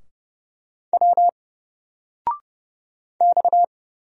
Morse code, heard as a keyed tone, about 30 wpm, 700 Hz. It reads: W X
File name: Single-Letter-Number-30wpm